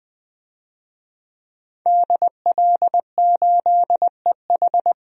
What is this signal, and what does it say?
Morse code, heard as a keyed tone, about 20 wpm, 700 Hz. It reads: DL8EH
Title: DXCC-100 Call Signs - 5 Characters - Repeated 2x - 20wpm